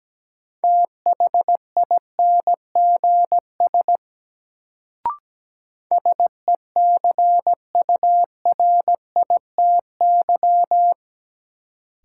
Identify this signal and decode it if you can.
Morse code, heard as a keyed tone, about 17 wpm, 700 Hz. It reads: THINGS SECURITY
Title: Sentences from Top 1000 Words - Review Part 4 - New Word Drill - 17wpm